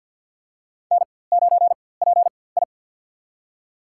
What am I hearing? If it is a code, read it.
Morse code, heard as a keyed tone, about 47 wpm, spaced as for 23 wpm, 700 Hz. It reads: N9PI
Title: USA Call Signs - 4x Character Spacing - 50wpm